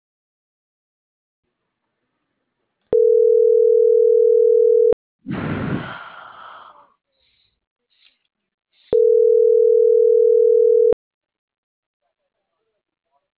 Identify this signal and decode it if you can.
Ringback tone (North America), 440 + 480 Hz.